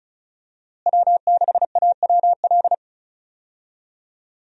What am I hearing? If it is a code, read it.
Morse code, heard as a keyed tone, about 34 wpm, 700 Hz. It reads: W6AWL
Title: USA Call Signs - 5 Characters - Repeated 2x - 35wpm